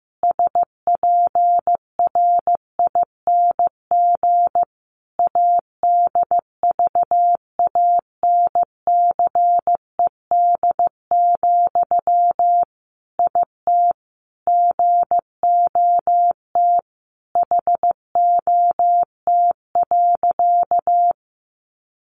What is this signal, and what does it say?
Morse code, heard as a keyed tone, about 15 wpm, 700 Hz. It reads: SPRING ADVANCED, IT GOT HOT.
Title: Sentences from Top 1200 Words 15wpm